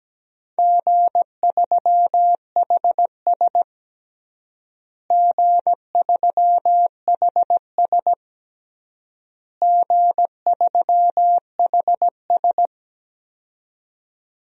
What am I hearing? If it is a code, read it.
Morse code, heard as a keyed tone, about 17 wpm, 700 Hz. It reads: G3HS G3HS G3HS
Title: DXCC-100 Call Signs - 4 Characters - Repeated 3x - 17wpm